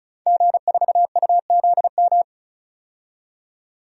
Morse code, heard as a keyed tone, about 34 wpm, 700 Hz. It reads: G4UZM
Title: DXCC-100 Call Signs - 5 Characters - 35wpm